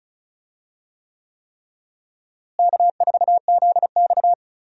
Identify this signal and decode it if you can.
Morse code, heard as a keyed tone, about 36 wpm, 700 Hz. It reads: K4ZX
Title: USA Call Signs - 4 Characters - Repeated 3x - 35wpm